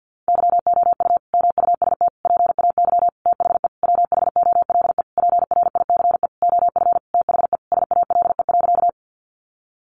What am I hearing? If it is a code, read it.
Morse code, heard as a keyed tone, about 50 wpm, 700 Hz. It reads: YOU MUST PAY THE WHOLE PRICE OF THE SALE.
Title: Sentences from Top 1400 Words 50wpm